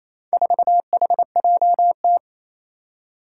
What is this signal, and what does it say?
Morse code, heard as a keyed tone, about 28 wpm, 700 Hz. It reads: 4HJT
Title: Four Challenging Letter-Number Combinations 28wpm